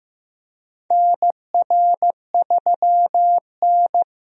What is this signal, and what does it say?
Morse code, heard as a keyed tone, about 15 wpm, 700 Hz. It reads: NR3N
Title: USA Call Signs - 4 Characters - Repeated 2x - 15wpm